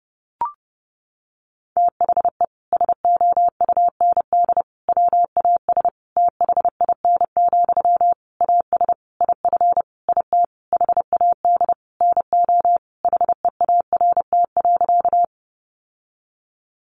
Morse code, heard as a keyed tone, about 30 wpm, 700 Hz. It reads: THE SOUND WAS THIN, AS IF IT HAD NO HEART.
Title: Sentences from Top 1200 Words 30wpm